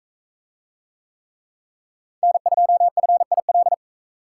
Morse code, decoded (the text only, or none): N2FIL